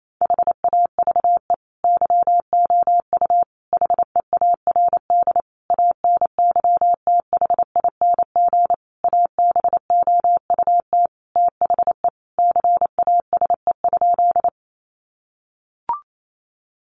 Morse code, heard as a keyed tone, about 28 wpm, 700 Hz. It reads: HAVE YOU HEARD ANYTHING ABOUT THE CASE?
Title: Sentences from Top 1000 Words 28wpm